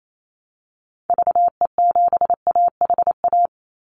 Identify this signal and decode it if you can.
Morse code, heard as a keyed tone, about 28 wpm, 700 Hz. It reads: VE7AHA
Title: Call Signs Repeated 2x - 28wpm